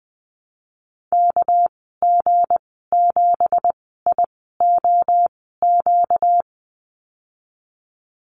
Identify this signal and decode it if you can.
Morse code, heard as a keyed tone, about 20 wpm, spaced as for 15 wpm, 700 Hz. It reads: KG7IOQ
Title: 238 Lesson 38g - Letter q - Call Signs 20wpm